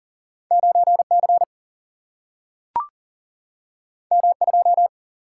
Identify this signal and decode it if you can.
Morse code, heard as a keyed tone, about 40 wpm, 700 Hz. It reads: 9 C M 2
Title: Single-Letter Single-Number 40wpm